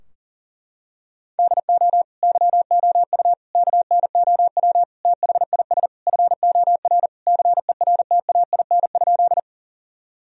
Morse code, heard as a keyed tone, about 40 wpm, 700 Hz. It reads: DO YOU KNOW THIS FOR CERTAIN?
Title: Sentences from Top 1000 Words - Review Part 2 - 40wpm